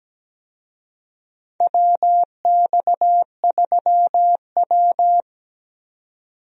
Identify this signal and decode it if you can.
Morse code, heard as a keyed tone, about 17 wpm, 700 Hz. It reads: WX3W
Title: USA Call Signs - 4 Characters - 17wpm